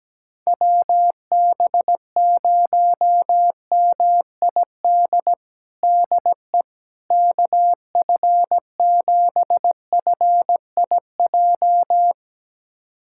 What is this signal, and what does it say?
Morse code, heard as a keyed tone, about 17 wpm, 700 Hz. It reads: WB0MID DE KF7FIJ
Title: USA Callsign DE Callsign - 6 Characters - 17wpm